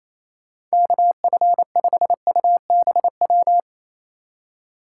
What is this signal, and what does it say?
Morse code, heard as a keyed tone, about 35 wpm, 700 Hz. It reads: KF5UBW